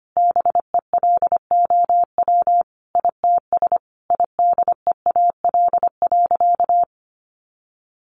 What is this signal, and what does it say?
Morse code, heard as a keyed tone, about 25 wpm, 700 Hz. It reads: BELOW ITS IDEAL.
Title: Sentences from Top 1400 Words 25wpm